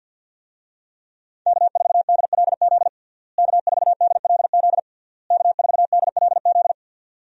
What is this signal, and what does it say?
Morse code, heard as a keyed tone, about 52 wpm, 700 Hz. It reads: K4DLZ K4DLZ K4DLZ
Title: USA Call Signs - 5 Characters - Repeated 3x - 50wpm